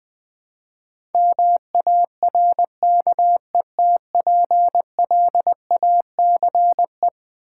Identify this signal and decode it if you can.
Morse code, heard as a keyed tone, about 20 wpm, 700 Hz. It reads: MARKETPLACE
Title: Build A Word - 4 Letters At A Time - 11-Letter Words - 20wpm